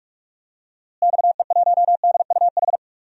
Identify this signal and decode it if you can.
Morse code, heard as a keyed tone, about 48 wpm, 700 Hz. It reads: XE1DUH
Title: DXCC-100 Call Signs - 6 Characters - Repeated 2x - 45wpm